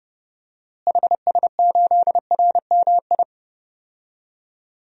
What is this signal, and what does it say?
Morse code, heard as a keyed tone, about 30 wpm, 700 Hz. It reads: HS8RMI
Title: DXCC-100 Call Signs - 6 Characters - Repeated 3x - 30wpm